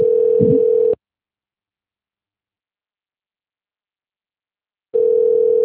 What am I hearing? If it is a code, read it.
Ringback tone (North America), 440 + 480 Hz.
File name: rg-1600-unknown- - - .15615